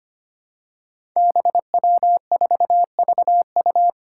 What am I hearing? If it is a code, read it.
Morse code, heard as a keyed tone, about 25 wpm, 700 Hz. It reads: BW4VU
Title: Five Challenging Letter-Number Combinations 25wpm